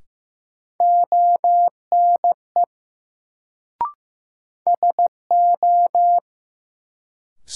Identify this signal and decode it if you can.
Morse code, heard as a keyed tone, about 15 wpm, 700 Hz. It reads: ONE SO
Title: Sets of 2 Words - New Word Drill - 15wpm